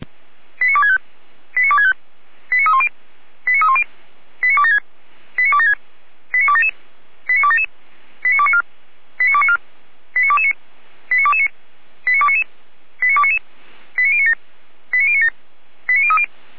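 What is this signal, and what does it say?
Five-tone selective call (ZVEI), heard as quick runs of five tones, tones of about 70 ms. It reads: 89265 89265 89310 89310 89276 89276 89277 89277 89294 89294 89229 89229 89299 89299 89097 89097 89030
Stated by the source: ZVEI five-tone selective call 89067 received by SDR, logged by Feuerwehr Orlamünde fire brigade.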